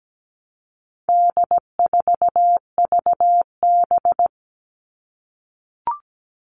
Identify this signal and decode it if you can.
Morse code, heard as a keyed tone, about 17 wpm, 700 Hz. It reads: D4VB
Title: All Cut Numbers - 4-Digit 17wpm